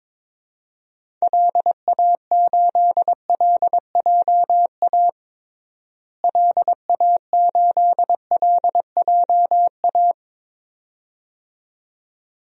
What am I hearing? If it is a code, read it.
Morse code, heard as a keyed tone, about 22 wpm, 700 Hz. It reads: LA8LJA LA8LJA
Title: DXCC-100 Call Signs - 6 Characters - Repeated 2x - 22wpm